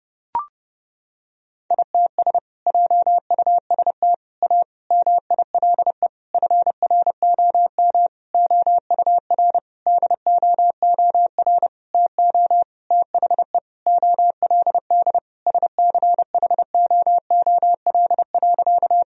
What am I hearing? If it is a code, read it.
Morse code, heard as a keyed tone, about 29 wpm, 700 Hz. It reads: ITS JUST A MILE FROM OUR DOOR TO THE OLD SCHOOL.
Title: Sentences from Top 1000 Words - Review Part 1 - 30wpm